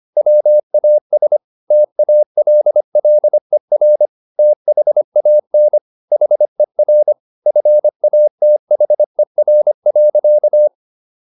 Morse code, heard as a keyed tone, about 25 wpm, 600 Hz. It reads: WAS TALLER THAN HER FATHER.